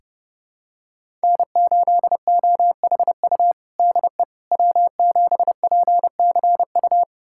Morse code, heard as a keyed tone, about 32 wpm, 700 Hz. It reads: N8OHU DE W7PCU
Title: USA Callsign DE Callsign - 5 Characters - 30wpm